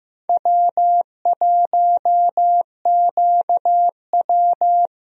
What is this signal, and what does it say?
Morse code, heard as a keyed tone, about 15 wpm, 700 Hz. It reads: W1QW
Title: USA Call Signs - 4 Characters - Repeated 2x - 15wpm